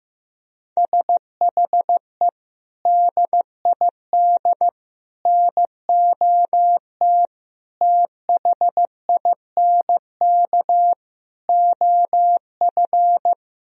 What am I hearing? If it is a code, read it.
Morse code, heard as a keyed tone, about 15 wpm, 700 Hz. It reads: SHE DID NOT THINK OF
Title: Sentences from Top 1000 Words - Review Part 3 - 15wpm